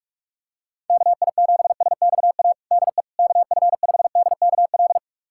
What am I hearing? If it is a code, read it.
Morse code, heard as a keyed tone, about 46 wpm, 700 Hz. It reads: KI7SXA DE KF5DKL
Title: USA Callsign DE Callsign - 6 Characters - 45wpm